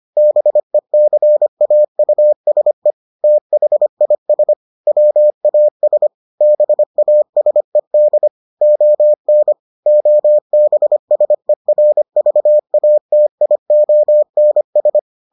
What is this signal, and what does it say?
Morse code, heard as a keyed tone, about 25 wpm, 600 Hz. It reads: BECAUSE THIS WAS BASED ON OBSERVATIONS